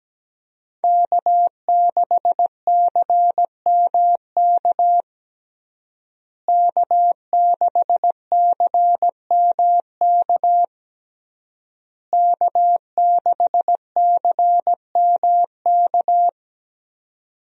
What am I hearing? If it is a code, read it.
Morse code, heard as a keyed tone, about 17 wpm, 700 Hz. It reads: K6CMK K6CMK K6CMK
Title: USA Call Signs - 5 Characters - Repeated 3x - 17wpm